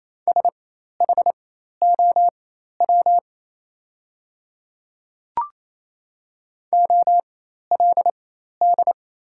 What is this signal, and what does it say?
Morse code, heard as a keyed tone, about 28 wpm, spaced as for 13 wpm, 700 Hz. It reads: SHOW OLD
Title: 4x Character Spacing - Top 200 Words - 28wpm